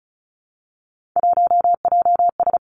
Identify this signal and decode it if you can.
Morse code, heard as a keyed tone, about 36 wpm, 700 Hz. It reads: 1JS
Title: Three Challenging Letter-Number Combinations 35wpm